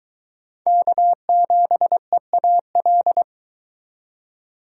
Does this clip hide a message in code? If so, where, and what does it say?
Yes: Morse code from 0.7 to 3.2 s, heard as a keyed tone, about 22 wpm, 700 Hz. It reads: K7EAL